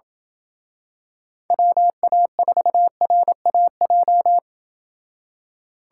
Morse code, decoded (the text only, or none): WA4RAJ